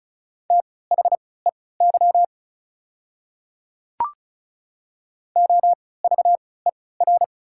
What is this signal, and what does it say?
Morse code, heard as a keyed tone, about 36 wpm, spaced as for 20 wpm, 690 Hz. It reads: THEY OVER
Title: 3x Character Spacing - Top 100 Words - 35wpm